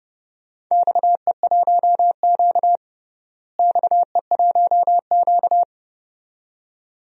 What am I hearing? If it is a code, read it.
Morse code, heard as a keyed tone, about 30 wpm, 700 Hz. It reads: XE1Q XE1Q
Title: DXCC-100 Call Signs - 4 Characters - Repeated 2x - 30wpm